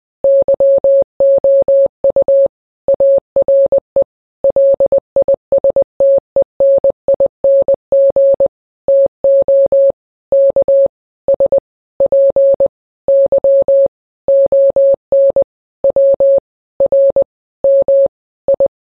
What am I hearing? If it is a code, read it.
Morse code, heard as a keyed tone, about 20 wpm, spaced as for 17 wpm, 550 Hz. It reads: YOU ARE LISTENING TO KSPY ON WRMI
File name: radio / Imaginary Stations / KSPY 9395 kHz 29.1.23